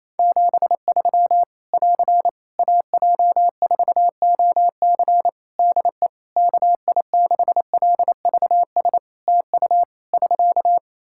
Morse code, heard as a keyed tone, about 28 wpm, 700 Hz. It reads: 73 <AR> AJ4OC DE KI6LVS TU <SK>
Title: Practice QSO - Exchange 4 - Ending - 28wpm